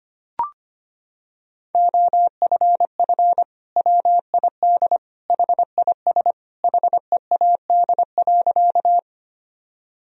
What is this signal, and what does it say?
Morse code, heard as a keyed tone, about 25 wpm, 700 Hz. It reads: OFF WID HIS HEAD.
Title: Sentences from Top 300 Words and Abbreviations 25wpm